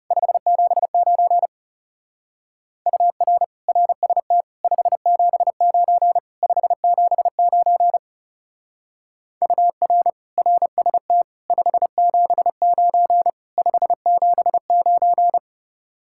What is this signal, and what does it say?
Morse code, heard as a keyed tone, about 36 wpm, 700 Hz. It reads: 579 UR RST 579 579 UR RST 579 579
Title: Speed-Racing Common QSO Phrases - 30wpm